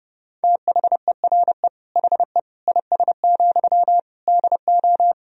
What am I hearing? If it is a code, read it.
Morse code, heard as a keyed tone, about 30 wpm, 700 Hz. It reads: THERE HE IS, DO
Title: Sentences from Top 100 Words and Abbreviations 30wpm